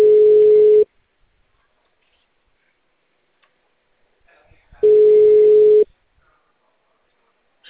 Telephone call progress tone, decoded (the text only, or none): ringback tone